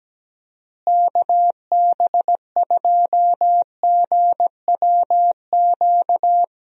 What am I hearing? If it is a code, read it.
Morse code, heard as a keyed tone, about 17 wpm, 700 Hz. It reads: KB2GWQ
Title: USA Call Signs - 6 Characters - Repeated 3x - 17wpm